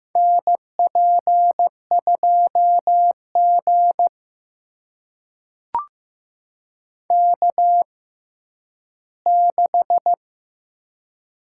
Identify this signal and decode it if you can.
Morse code, heard as a keyed tone, about 15 wpm, spaced as for 10 wpm, 700 Hz. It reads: NP2G K6
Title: USA Call Signs - 4 Characters - 6x Character Spacing - 15wpm